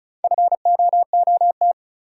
Morse code, heard as a keyed tone, about 34 wpm, 700 Hz. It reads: FOOT
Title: Sentences from Top 500 Words - New Word Drill - 35wpm